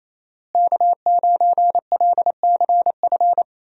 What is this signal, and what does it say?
Morse code, heard as a keyed tone, about 28 wpm, 700 Hz. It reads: K9LCF
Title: USA Call Signs - 5 Characters - Repeated 2x - 28wpm